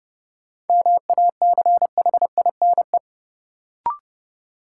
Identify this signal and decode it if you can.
Morse code, heard as a keyed tone, about 30 wpm, 700 Hz. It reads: MACHINE
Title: Sentences from Top 1000 Words - Review Part 2 - New Word Drill - 30wpm